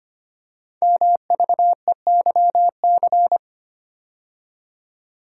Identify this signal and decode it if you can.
Morse code, heard as a keyed tone, about 25 wpm, 700 Hz. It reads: MVEYC
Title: Code Groups - 5 Letters - 25wpm